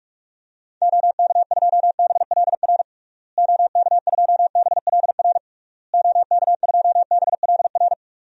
Morse code, heard as a keyed tone, about 46 wpm, 700 Hz. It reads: OK2BLR OK2BLR OK2BLR
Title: DXCC-100 Call Signs - 6 Characters - Repeated 3x - 45wpm